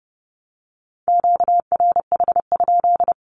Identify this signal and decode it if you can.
Morse code, heard as a keyed tone, about 30 wpm, 700 Hz. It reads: QRH?
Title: Q-Codes 30wpm